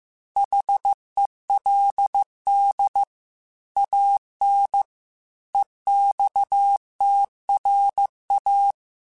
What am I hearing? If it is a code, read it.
Morse code, heard as a keyed tone, about 15 wpm, 800 Hz. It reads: HELD AN EXTRA